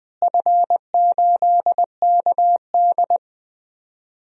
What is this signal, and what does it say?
Morse code, heard as a keyed tone, about 20 wpm, 700 Hz. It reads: F8KD
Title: DXCC-100 Call Signs - 4 Characters - Repeated 2x - 20wpm